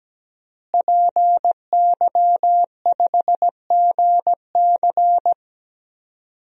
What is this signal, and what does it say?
Morse code, heard as a keyed tone, about 17 wpm, 700 Hz. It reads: PY5GC
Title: DXCC-100 Call Signs - 5 Characters - Repeated 2x - 17wpm